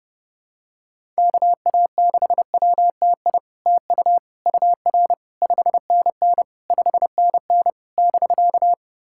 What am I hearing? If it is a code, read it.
Morse code, heard as a keyed tone, about 30 wpm, 700 Hz. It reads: KA6WTI TU UR 5NN 5NN <BK>
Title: SOTA QSO Practice 30wpm